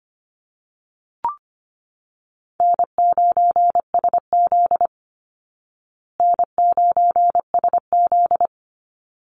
Morse code, heard as a keyed tone, about 25 wpm, 700 Hz. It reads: N9SZ N9SZ
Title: Easy Call Signs Repeated 2x - 25wpm